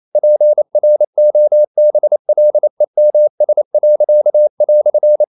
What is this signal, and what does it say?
Morse code, heard as a keyed tone, about 28 wpm, 600 Hz. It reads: PROBLEMS."
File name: CW / Practice / EnglishQuotations0128